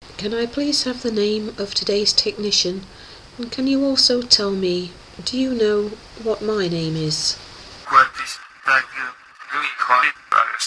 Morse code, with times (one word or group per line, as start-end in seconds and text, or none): none